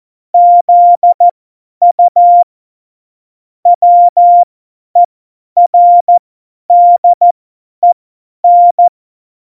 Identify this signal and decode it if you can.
Morse code, heard as a keyed tone, about 14 wpm, spaced as for 10 wpm, 700 Hz. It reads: ZU WERDEN
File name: Morsekurs / Witze / Witz 07. November 2017, 50 F70